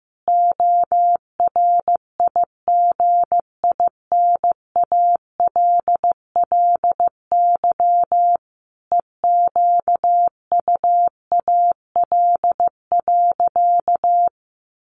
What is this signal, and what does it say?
Morse code, heard as a keyed tone, about 15 wpm, 700 Hz. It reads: ORIGINALLY EQUAL.
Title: Sentences from Top 1500 Words 15wpm